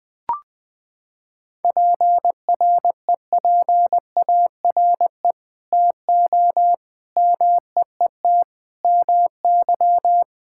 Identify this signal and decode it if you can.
Morse code, heard as a keyed tone, about 20 wpm, 700 Hz. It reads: PREPARE TO MEET MY
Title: Sentences from Top 1000 Words - Review Part 6 - 20wpm